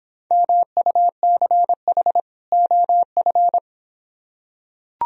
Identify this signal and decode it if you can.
Morse code, heard as a keyed tone, about 26 wpm, 700 Hz. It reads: MUCH OF